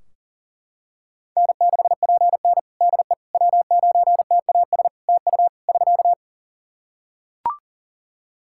Morse code, heard as a keyed tone, about 37 wpm, 700 Hz. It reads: N6PN DE W9TAS TU <SK>
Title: Practice QSO - Exchange 4 - Ending - 40wpm